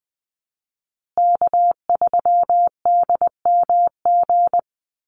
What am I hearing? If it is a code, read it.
Morse code, heard as a keyed tone, about 20 wpm, 700 Hz. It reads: K3DMG
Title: Call Signs Repeated 2x - 20wpm